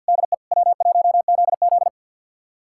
Morse code, heard as a keyed tone, about 47 wpm, 700 Hz. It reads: DE W1ZZ
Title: USA Callsign DE Callsign - 4 Characters - 50wpm